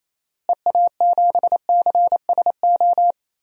Morse code, heard as a keyed tone, about 28 wpm, 700 Hz. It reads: EA7CSO
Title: DXCC-100 Call Signs - 6 Characters - Repeated 3x - 28wpm